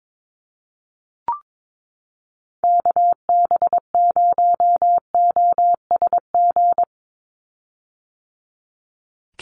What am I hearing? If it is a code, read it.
Morse code, heard as a keyed tone, about 22 wpm, 700 Hz. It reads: KB0OSG
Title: USA Call Signs 22wpm